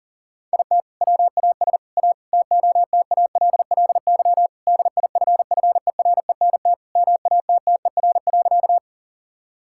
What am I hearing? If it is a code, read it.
Morse code, heard as a keyed tone, about 40 wpm, 700 Hz. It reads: IT WAS A TOTALLY DIFFERENT MATTER.
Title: Sentences from Top 1600 Words 40wpm